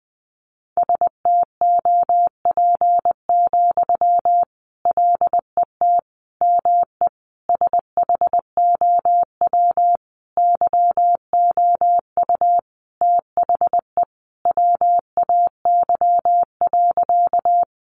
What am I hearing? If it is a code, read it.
Morse code, heard as a keyed tone, about 20 wpm, 700 Hz. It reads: STOP, LET ME SHOW YOU THE WAY.
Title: Sentences from Top 300 Words 20wpm